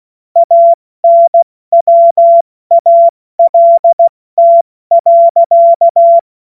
Morse code, heard as a keyed tone, about 16 wpm, 680 Hz. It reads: ANWALT.
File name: Morsekurs / Witze / Witz 04. Oktober 2022 70 F80